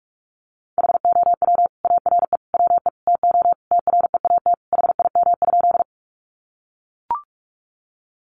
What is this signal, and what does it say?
Morse code, heard as a keyed tone, about 44 wpm, 700 Hz. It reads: HOW ARE WE TO TREAT HIM?
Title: Sentences from Top 1300 Words 45wpm